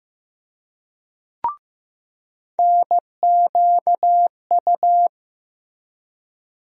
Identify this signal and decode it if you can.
Morse code, heard as a keyed tone, about 15 wpm, 700 Hz. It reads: NQU
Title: USA Call Sign Postfixes - 3 Letters 15wpm